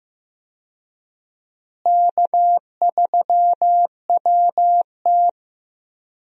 Morse code, heard as a keyed tone, about 15 wpm, 700 Hz. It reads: K3WT
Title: USA Call Signs - 4 Characters - Repeated 3x - 15wpm